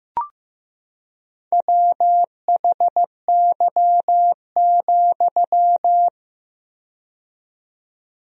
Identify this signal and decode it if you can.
Morse code, heard as a keyed tone, about 15 wpm, 700 Hz. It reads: WHY,
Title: 4x Word-Spacing - Sentences from Top 300 Words - 15wpm